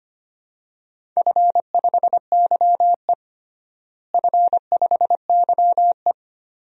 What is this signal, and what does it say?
Morse code, heard as a keyed tone, about 25 wpm, 700 Hz. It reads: F5YE F5YE
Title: DXCC-100 Call Signs - 4 Characters - Repeated 3x - 25wpm